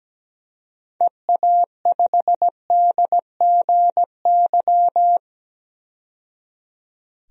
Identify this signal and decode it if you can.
Morse code, heard as a keyed tone, about 17 wpm, 700 Hz. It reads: EA5DGY
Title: DXCC-100 Call Signs - 6 Characters - 17wpm